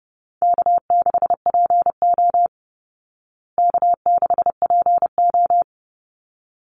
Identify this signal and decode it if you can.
Morse code, heard as a keyed tone, about 30 wpm, 700 Hz. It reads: K6PO K6PO
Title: Easy Call Signs Repeated 2x - 30wpm